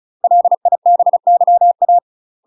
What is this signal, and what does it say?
Morse code, heard as a keyed tone, about 36 wpm, 700 Hz. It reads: LIBYA
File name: UK MASTER QSOs TRAINING / 35WPM / UK MASTER QSOs 0028